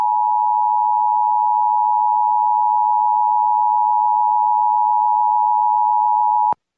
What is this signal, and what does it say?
Emergency Alert System (SAME) tones (the 853 and 960 Hz attention tone). It is heard from 0.2 to 5.9 s.